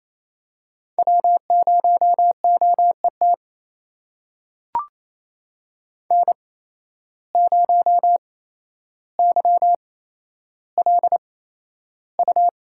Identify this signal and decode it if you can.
Morse code, heard as a keyed tone, about 28 wpm, spaced as for 12 wpm, 700 Hz. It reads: W0OET N0YLU
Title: USA Call Signs - 5 Characters - 8x Character Spacing - 28wpm